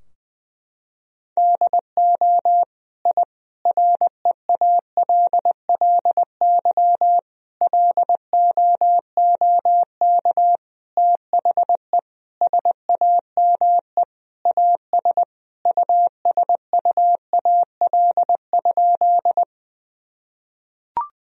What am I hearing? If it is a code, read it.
Morse code, heard as a keyed tone, about 21 wpm, 700 Hz. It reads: DO I REALLY LOOK THE SAME AS USUAL?
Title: Sentences from Top 1000 Words - Review Part 2 - 20wpm